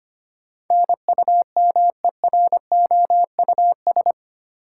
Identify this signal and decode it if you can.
Morse code, heard as a keyed tone, about 25 wpm, 700 Hz. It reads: NUMEROUS
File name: Eight Letter Words - 25wpm